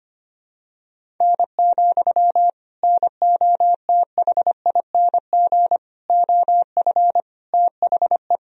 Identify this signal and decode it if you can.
Morse code, heard as a keyed tone, about 25 wpm, 700 Hz. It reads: N, NOTHING OF THE
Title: Sentences from Top 500 Words and Abbreviations 25wpm